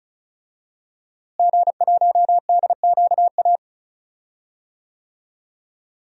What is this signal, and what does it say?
Morse code, heard as a keyed tone, about 36 wpm, 700 Hz. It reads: G1DQA
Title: DXCC-100 Call Signs - 5 Characters - Repeated 2x - 35wpm